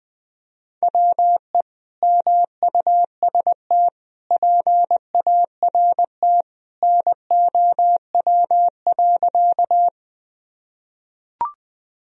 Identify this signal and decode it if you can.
Morse code, heard as a keyed tone, about 21 wpm, 700 Hz. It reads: WE MUST PART NOW.
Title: Sentences from Top 1000 Words - Review Part 1 - 20wpm